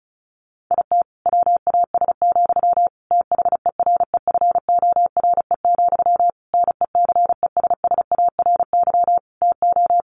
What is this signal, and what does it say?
Morse code, heard as a keyed tone, about 34 wpm, 700 Hz. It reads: IT WAS, THEREFORE, NECESSARY TO
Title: Sentences from Top 1500 Words 35wpm